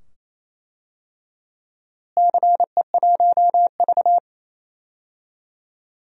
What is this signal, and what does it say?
Morse code, heard as a keyed tone, about 28 wpm, 700 Hz. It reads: CE1V
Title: DXCC-100 Call Signs - 4 Characters - Repeated 2x - 28wpm